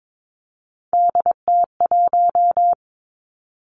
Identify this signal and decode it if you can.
Morse code, heard as a keyed tone, about 22 wpm, 700 Hz. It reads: DT1
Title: Three Challenging Letter-Number Combinations 22wpm